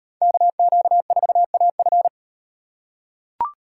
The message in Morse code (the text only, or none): KQ4AF